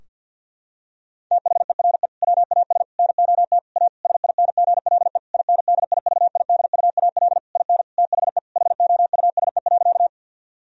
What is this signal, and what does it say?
Morse code, heard as a keyed tone, about 50 wpm, 700 Hz. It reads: THERE WAS NOT A SINGLE INDIVIDUAL IN THE HOUSE.
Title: Sentences from Top 1000 Words - Review Part 4 - 50wpm